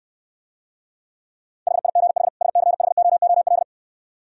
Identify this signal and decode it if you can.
Morse code, heard as a keyed tone, about 73 wpm, 700 Hz. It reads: HIGH SCHOOL